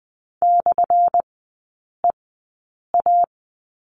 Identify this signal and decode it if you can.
Morse code, heard as a keyed tone, about 20 wpm, spaced as for 8 wpm, 700 Hz. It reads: /EA
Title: 178 Lesson 29e - Slash - Three Characters 20wpm